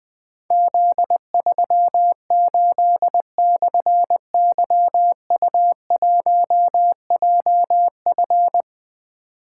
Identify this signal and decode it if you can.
Morse code, heard as a keyed tone, about 20 wpm, 700 Hz. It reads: Z38/YU1JF
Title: Difficult Call Signs 20wpm